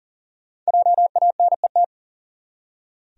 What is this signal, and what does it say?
Morse code, heard as a keyed tone, about 40 wpm, 700 Hz. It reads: JANET
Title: USA Common Names - 40wpm